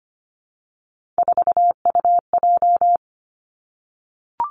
Morse code, heard as a keyed tone, about 25 wpm, 700 Hz. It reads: 4UJ
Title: Three Challenging Letter-Number Combinations 25wpm